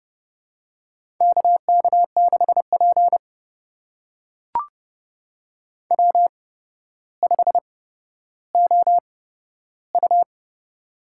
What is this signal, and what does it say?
Morse code, heard as a keyed tone, about 32 wpm, spaced as for 13 wpm, 700 Hz. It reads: KK6P W5OU
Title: USA Call Signs - 4 Characters - 8x Character Spacing - 30wpm